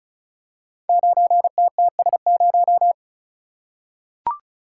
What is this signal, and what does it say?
Morse code, heard as a keyed tone, about 36 wpm, 700 Hz. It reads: 9TTS0
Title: Code Groups - 5 Letter:Numbers - 35wpm